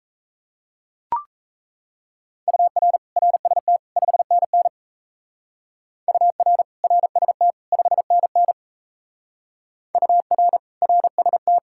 Morse code, heard as a keyed tone, about 32 wpm, 700 Hz. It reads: UR RST 5NN UR RST 5NN UR RST
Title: Speed-Racing Common QSO Phrases - 28wpm